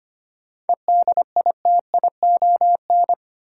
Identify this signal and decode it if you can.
Morse code, heard as a keyed tone, about 25 wpm, 700 Hz. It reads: EDITION
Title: Sets of 5 Words - New Word Drill - 25wpm